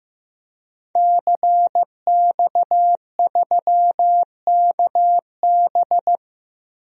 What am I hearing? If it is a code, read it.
Morse code, heard as a keyed tone, about 15 wpm, 700 Hz. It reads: CX3KB
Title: DXCC-100 Call Signs - 5 Characters - 15wpm